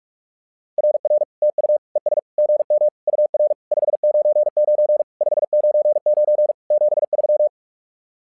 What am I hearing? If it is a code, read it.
Morse code, heard as a keyed tone, about 46 wpm, 600 Hz. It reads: RR TU ES GM UR 599 599 73
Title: SOTA QSO Practice 45wpm